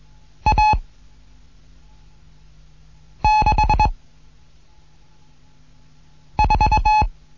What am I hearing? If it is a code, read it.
Morse code, heard as a keyed tone, about 21 wpm, 860 Hz. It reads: A 6 4
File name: Morsen / Kurs A 15